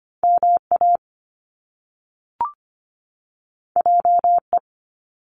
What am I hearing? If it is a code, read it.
Morse code, heard as a keyed tone, about 25 wpm, 700 Hz. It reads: MA JE
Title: Top Two Letter Combinations in Words 25wpm